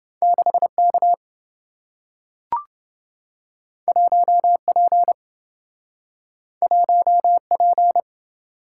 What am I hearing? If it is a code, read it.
Morse code, heard as a keyed tone, about 27 wpm, 700 Hz. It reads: 6K 1P 1P